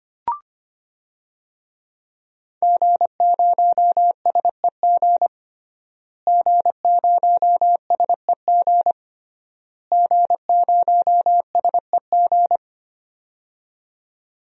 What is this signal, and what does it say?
Morse code, heard as a keyed tone, about 25 wpm, 700 Hz. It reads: G0SEG G0SEG G0SEG
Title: DXCC-100 Call Signs - 5 Characters - Repeated 3x - 25wpm